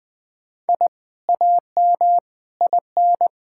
Morse code, heard as a keyed tone, about 20 wpm, 700 Hz. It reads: I AM IN
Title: Sentences from Top 1000 Words - Review Part 6 - 20wpm